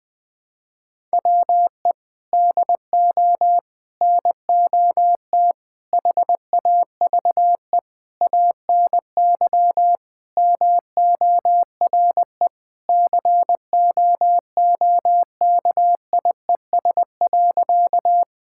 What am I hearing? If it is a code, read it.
Morse code, heard as a keyed tone, about 19 wpm, 700 Hz. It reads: WE DO NOT HAVE ANY MORE COOKIES.
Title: Sentences with 5-Word Phrases - 20wpm